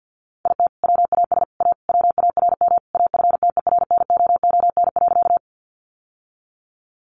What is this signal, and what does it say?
Morse code, heard as a keyed tone, about 51 wpm, 700 Hz. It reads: IT WAS A WARM AFTERNOON.
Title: Sentences from Top 1500 Words 50wpm